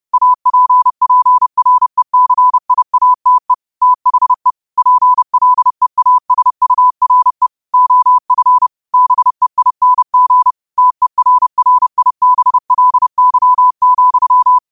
Morse code, heard as a keyed tone, about 30 wpm, 1000 Hz. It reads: APPRECIATE THE PLEASURE OF BEING TERRIBLY,